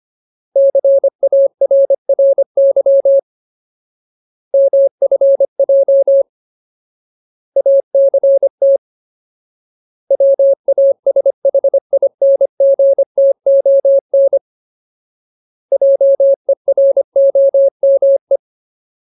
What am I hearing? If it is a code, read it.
Morse code, heard as a keyed tone, about 25 wpm, 550 Hz. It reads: CARRY MFJ ACT WASHINGTON JEROME
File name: Ultimate Word List / 25WPM Ultimate